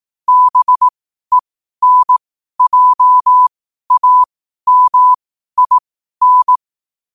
Morse code, heard as a keyed tone, about 18 wpm, spaced as for 12 wpm, 1000 Hz. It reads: BENJAMIN
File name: qotdfiles / 12wpm